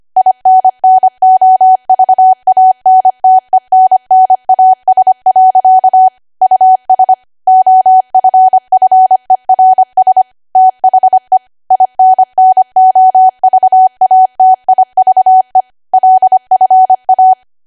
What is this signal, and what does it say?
Morse code, heard as a keyed tone, about 25 wpm, 750 Hz. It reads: INNOVANTENNAS. US OFFERS THE INNOVATIVE LFA
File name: Archive / 25 WPM / 25WPM